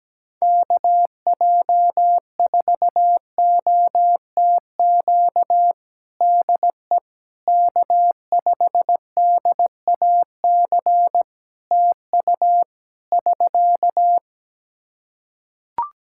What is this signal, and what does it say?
Morse code, heard as a keyed tone, about 17 wpm, 700 Hz. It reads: KJ4OTQ DE K5DAC TU <SK>
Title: Practice QSO - Exchange 4 - Ending - 17wpm